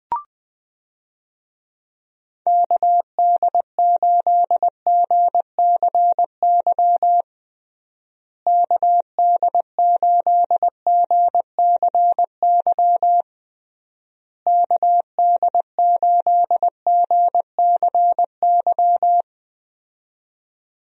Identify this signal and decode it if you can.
Morse code, heard as a keyed tone, about 20 wpm, 700 Hz. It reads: KD8GCY KD8GCY KD8GCY
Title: USA Call Signs - 6 Characters - Repeated 3x - 20wpm